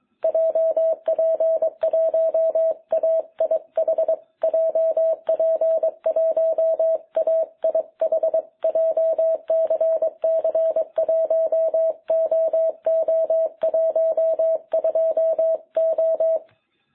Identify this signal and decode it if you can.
Morse code, heard as a keyed tone, about 22 wpm, 660 Hz. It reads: JP1AIH JP1AIH JCC1OO12O
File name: jh8cbh / 2021cw